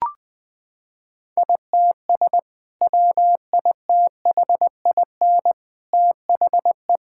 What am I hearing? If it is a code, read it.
Morse code, heard as a keyed tone, about 20 wpm, 700 Hz. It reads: ITS WITHIN THE
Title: Sentences with 2-Word Phrases - 20wpm